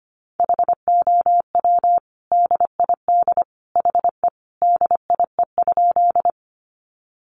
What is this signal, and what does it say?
Morse code, heard as a keyed tone, about 25 wpm, 700 Hz. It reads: HOW DID HE DIE?
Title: Sentences from Top 1300 Words 25wpm